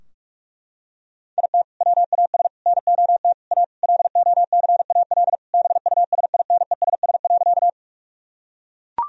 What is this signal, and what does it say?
Morse code, heard as a keyed tone, about 41 wpm, 700 Hz. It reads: IT WAS NOT A LOCAL BUSINESS.
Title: Sentences from Top 1000 Words - Review Part 3 - 45wpm